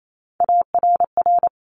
Morse code, heard as a keyed tone, about 28 wpm, 700 Hz. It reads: ARR
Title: Top Three Letter Combinations in Words 28wpm